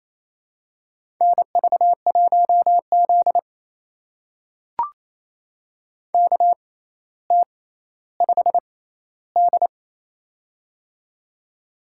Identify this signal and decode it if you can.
Morse code, heard as a keyed tone, about 28 wpm, spaced as for 14 wpm, 700 Hz. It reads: NV1Z KT5D
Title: USA Call Signs - 4 Characters - 6x Character Spacing - 28wpm